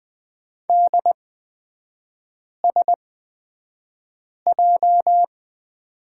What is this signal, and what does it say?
Morse code, heard as a keyed tone, about 20 wpm, spaced as for 5 wpm, 700 Hz. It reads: DSJ
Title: 236 Lesson 38e - Letter q - Three Characters 20wpm